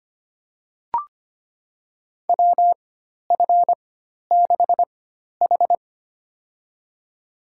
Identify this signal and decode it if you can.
Morse code, heard as a keyed tone, about 25 wpm, spaced as for 12 wpm, 690 Hz. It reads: WF6H
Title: USA Call Signs - 4 Characters - 4x Character Spacing - 25wpm